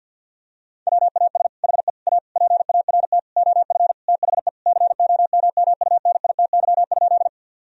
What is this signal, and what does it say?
Morse code, heard as a keyed tone, about 50 wpm, 700 Hz. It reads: WAS HE A PART OF THE COMMUNITY?
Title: Sentences from Top 1000 Words - Review Part 4 - 50wpm